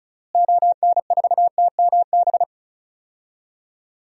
Morse code, heard as a keyed tone, about 36 wpm, 700 Hz. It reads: ON4TMB
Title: DXCC-100 Call Signs - 6 Characters - Repeated 2x - 35wpm